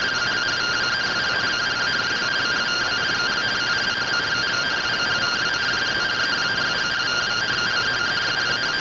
Two-tone teleprinter signal, about 170 Hz shift.
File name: findings / CORONAL fr69 nb202 3mm.gen-10 rtty rx